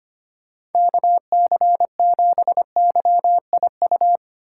Morse code, heard as a keyed tone, about 32 wpm, 700 Hz. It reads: KC7YIU